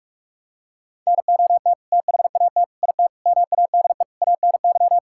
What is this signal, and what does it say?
Morse code, heard as a keyed tone, about 43 wpm, 700 Hz. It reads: NOT THAT IT MADE ANY
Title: Sentences from Top 1000 Words - Review Part 5 - 45wpm